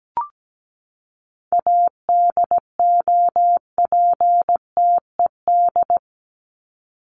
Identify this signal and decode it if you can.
Morse code, heard as a keyed tone, about 17 wpm, 700 Hz. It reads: ADOPTED
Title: ED Suffix - Top 100 Words with Base Words 17wpm